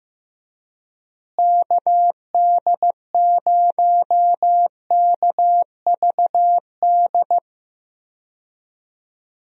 Morse code, heard as a keyed tone, about 15 wpm, 700 Hz. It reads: KD0KVD
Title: USA Call Signs - 6 Characters - Repeated 2x - 15wpm